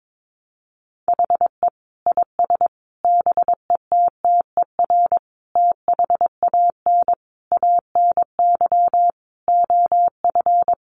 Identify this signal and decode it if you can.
Morse code, heard as a keyed tone, about 22 wpm, 700 Hz. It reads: HE IS BETTER THAN ANY OF
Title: Sentences from Top 400 Words 22wpm